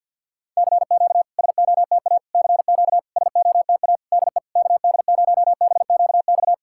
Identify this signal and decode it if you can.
Morse code, heard as a keyed tone, about 49 wpm, 690 Hz. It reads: CQ SOTA CQ SOTA DE KD0BQX
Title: SOTA QSO Practice 50wpm